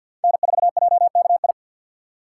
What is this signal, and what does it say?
Morse code, heard as a keyed tone, about 48 wpm, 700 Hz. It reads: N4JKI
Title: USA Call Signs - 4x Character Spacing - 50wpm